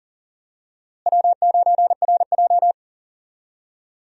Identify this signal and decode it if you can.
Morse code, heard as a keyed tone, about 39 wpm, 700 Hz. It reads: W9RJ
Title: USA Call Signs - 4 Characters - Repeated 3x - 40wpm